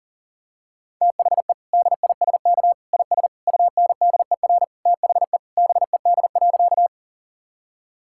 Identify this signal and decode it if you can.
Morse code, heard as a keyed tone, about 40 wpm, 700 Hz. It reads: THE DISK IS UNDER THE BED.
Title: Sentences from Top 1000 Words - Review Part 6 - 40wpm